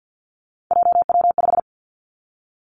Morse code, heard as a keyed tone, about 50 wpm, 700 Hz. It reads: JW5
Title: Three Challenging Letter-Number Combinations 50wpm